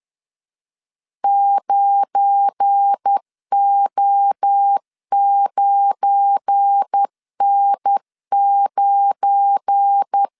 Morse code, heard as a keyed tone, about 10 wpm, 800 Hz. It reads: 9O9N9
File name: LEZIONI / 7 GF9Z / 7 GF9 10wpm 2747